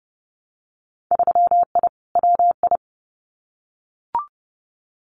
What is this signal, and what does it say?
Morse code, heard as a keyed tone, about 30 wpm, 700 Hz. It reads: 3I WI
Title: Winter Field Day Exchanges 30wpm